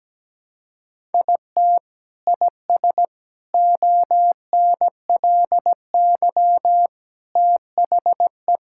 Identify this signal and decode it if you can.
Morse code, heard as a keyed tone, about 17 wpm, 700 Hz. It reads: IT IS ONLY THE
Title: Sentences from Top 1000 Words - Review Part 5 - 17wpm